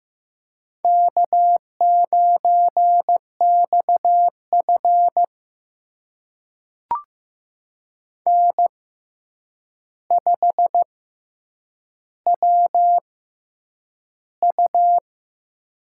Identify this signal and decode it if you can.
Morse code, heard as a keyed tone, about 15 wpm, spaced as for 8 wpm, 700 Hz. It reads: K9XF N5WU
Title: USA Call Signs - 4 Characters - 6x Character Spacing - 15wpm